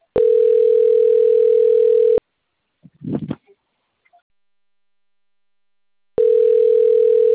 Ringback tone (North America), 440 + 480 Hz.